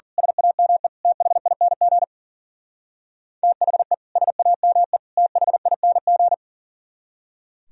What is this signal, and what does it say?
Morse code, heard as a keyed tone, about 43 wpm, 700 Hz. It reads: SAME THING THE SAME THING
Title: Speed-Racing - Sets of 3 Words 40wpm